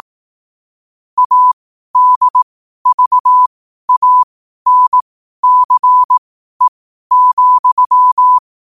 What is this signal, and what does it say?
Morse code, heard as a keyed tone, about 18 wpm, spaced as for 12 wpm, 1000 Hz. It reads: ADVANCE,
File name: qotdfiles / 12wpm